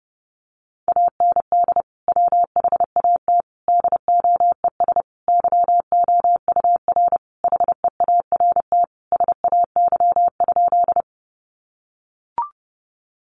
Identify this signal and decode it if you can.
Morse code, heard as a keyed tone, about 30 wpm, 700 Hz. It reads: AND WHAT DOES YOUR HEART SAY?
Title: Sentences from Top 1200 Words 30wpm